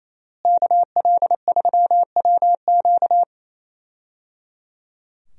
Morse code, heard as a keyed tone, about 28 wpm, 700 Hz. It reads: KL3WQ
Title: USA Call Signs - 5 Characters - Repeated 3x - 28wpm